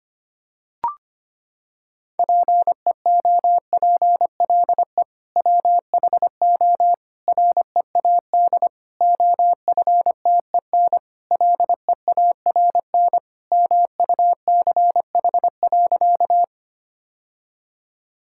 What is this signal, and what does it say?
Morse code, heard as a keyed tone, about 25 wpm, 700 Hz. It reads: PEOPLE WHO READ OFTEN LEARN MUCH.
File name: practice / 25wpm / Sentences with 2-Word Phrases - 25wpm